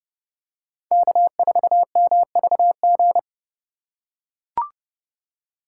Morse code, heard as a keyed tone, about 30 wpm, 700 Hz. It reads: K4MVG